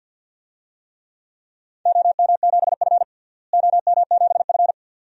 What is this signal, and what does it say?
Morse code, heard as a keyed tone, about 50 wpm, 700 Hz. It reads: OM7F OM7F
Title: DXCC-100 Call Signs - 4 Characters - Repeated 2x - 50wpm